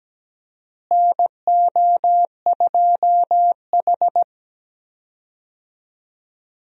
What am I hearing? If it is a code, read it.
Morse code, heard as a keyed tone, about 17 wpm, 700 Hz. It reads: NO2H
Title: USA Call Signs - 4 Characters - Repeated 2x - 17wpm